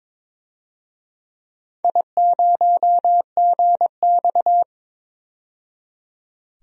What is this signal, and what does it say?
Morse code, heard as a keyed tone, about 22 wpm, 700 Hz. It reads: I0GX
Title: DXCC-100 Call Signs - 4 Characters - 22wpm